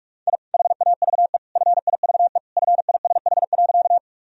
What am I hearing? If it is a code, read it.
Morse code, heard as a keyed tone, about 46 wpm, 700 Hz. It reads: I HAVE FIVE FISH.
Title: Sentences from Top 1000 Words - Review Part 2 - 45wpm